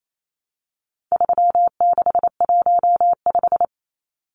Morse code, heard as a keyed tone, about 28 wpm, 700 Hz. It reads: 3615
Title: Numbers - 4-Digit 28wpm